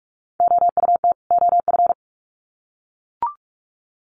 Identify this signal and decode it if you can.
Morse code, heard as a keyed tone, about 44 wpm, 700 Hz. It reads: OUT OF
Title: Sets of 2 Words 45wpm